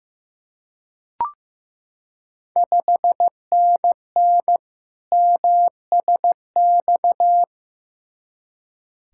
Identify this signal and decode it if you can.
Morse code, heard as a keyed tone, about 15 wpm, 700 Hz. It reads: 5NN MSX
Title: Ontario QSO Party Exchange 15wpm